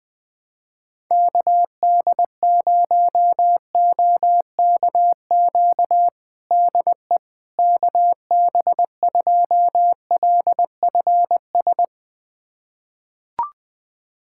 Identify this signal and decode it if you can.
Morse code, heard as a keyed tone, about 21 wpm, 700 Hz. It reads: KD0OKQ DE KB2LFS
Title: USA Callsign DE Callsign - 6 Characters - 20wpm